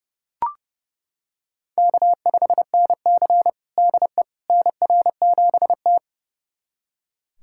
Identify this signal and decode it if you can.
Morse code, heard as a keyed tone, about 32 wpm, 700 Hz. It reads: K5NC DE NR7T
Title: USA Callsign DE Callsign - 4 Characters - 30wpm